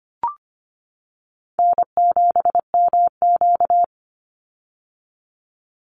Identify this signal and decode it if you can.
Morse code, heard as a keyed tone, about 25 wpm, 700 Hz. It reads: N7MQ
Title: Easy Call Signs - 25wpm